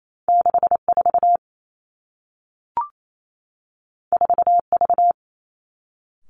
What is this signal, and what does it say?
Morse code, heard as a keyed tone, about 28 wpm, 700 Hz. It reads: 6 4 4 V
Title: Two Challenging Letter-Number Combinations 28wpm